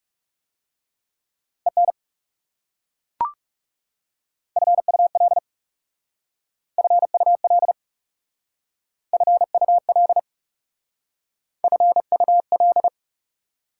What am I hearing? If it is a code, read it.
Morse code, heard as a keyed tone, about 36 wpm, 700 Hz. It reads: EN FUL FUL FUL FUL
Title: Speed-Racing Prefixes and Suffixes 30wpm